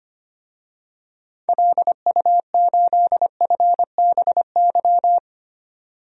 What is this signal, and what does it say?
Morse code, heard as a keyed tone, about 25 wpm, 700 Hz. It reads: LU8FBY
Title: DXCC-100 Call Signs - 6 Characters - 25wpm